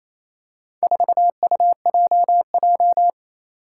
Morse code, heard as a keyed tone, about 27 wpm, 700 Hz. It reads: 4UJJ
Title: Four Challenging Letter-Number Combinations 28wpm